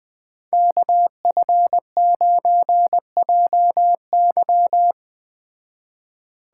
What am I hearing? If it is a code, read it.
Morse code, heard as a keyed tone, about 20 wpm, 700 Hz. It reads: KF9JY
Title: Speed Racing - USA 2x2 Call Signs 17wpm